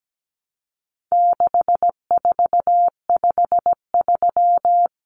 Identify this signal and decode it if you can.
Morse code, heard as a keyed tone, about 17 wpm, 700 Hz. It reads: 6453
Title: Numbers - 4-Digit 17wpm